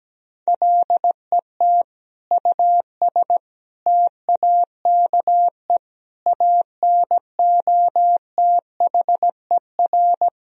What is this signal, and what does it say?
Morse code, heard as a keyed tone, about 17 wpm, 700 Hz. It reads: LET US TAKE ANOTHER
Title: Sentences from Top 1000 Words - Review Part 2 - 17wpm